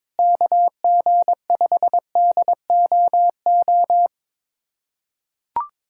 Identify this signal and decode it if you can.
Morse code, heard as a keyed tone, about 21 wpm, 700 Hz. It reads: KG5DOO
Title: USA Callsign DE Callsign - 6 Characters - 22wpm